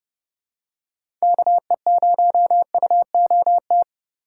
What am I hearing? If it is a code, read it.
Morse code, heard as a keyed tone, about 32 wpm, 700 Hz. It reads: KE0UOT
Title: USA Call Signs - 6 Characters - 3x Character Spacing - 30wpm